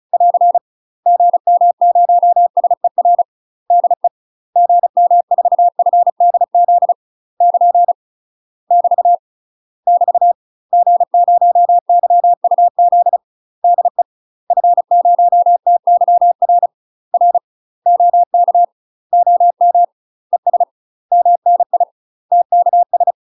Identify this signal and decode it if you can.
Morse code, heard as a keyed tone, about 34 wpm, 700 Hz. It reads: <AR> GM0SER DE GM4FDZ <KN> <BT> <BT> G0YUZ DE F0TYR R OK OM ES MNI TKS
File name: UK MASTER QSOs TRAINING / 35WPM / UK MASTER QSOs 0041